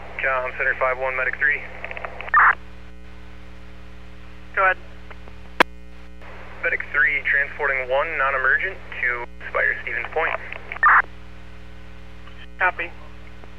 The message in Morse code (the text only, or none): none